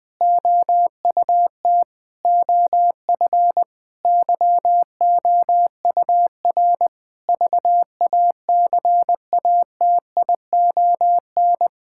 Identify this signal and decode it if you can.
Morse code, heard as a keyed tone, about 20 wpm, 700 Hz. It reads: OUT OF YOUR VACATION
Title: Sentences with 7-Word Phrases - 20wpm